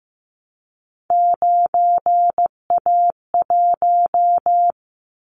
Morse code, heard as a keyed tone, about 15 wpm, 700 Hz. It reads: 9A1
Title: Callsign Prefixes for Top 25 DXCC - Repeated 3x - 15wpm